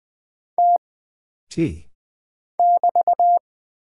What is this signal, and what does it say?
Morse code, heard as a keyed tone, about 20 wpm, 700 Hz. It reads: T <BT>
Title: Single Character (Letter:Number:Punctuation:Prosign) - Mind-Melt - 20wpm